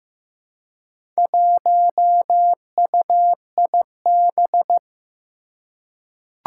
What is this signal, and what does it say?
Morse code, heard as a keyed tone, about 15 wpm, 700 Hz. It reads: 1UIB
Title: Four Challenging Letter-Number Combinations 15wpm